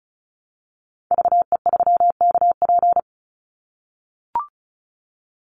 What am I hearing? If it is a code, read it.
Morse code, heard as a keyed tone, about 34 wpm, 700 Hz. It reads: VE3KP
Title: Call Signs - 35wpm